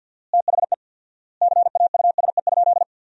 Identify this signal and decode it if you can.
Morse code, heard as a keyed tone, about 50 wpm, 700 Hz. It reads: THE CAUSE?
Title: 4x Word-Spacing - Sentences from Top 200 Words - 50wpm